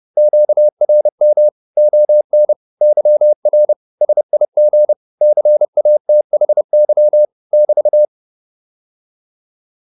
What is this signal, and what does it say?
Morse code, heard as a keyed tone, about 30 wpm, 600 Hz. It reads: QRM ON YR SIG CATHY <BT>